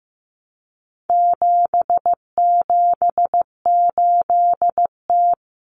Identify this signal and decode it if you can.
Morse code, heard as a keyed tone, about 15 wpm, 700 Hz. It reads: 778T
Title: Common Cut Numbers - 4-Digit 15wpm